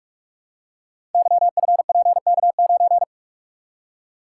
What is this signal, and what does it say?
Morse code, heard as a keyed tone, about 44 wpm, 700 Hz. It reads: YFPK9
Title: Code Groups - 5 Letter:Numbers - 45wpm